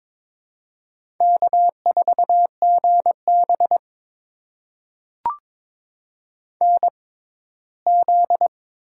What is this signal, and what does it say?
Morse code, heard as a keyed tone, about 22 wpm, spaced as for 15 wpm, 700 Hz. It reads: K4GB NZ
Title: USA Call Signs - 4 Characters - 6x Character Spacing - 22wpm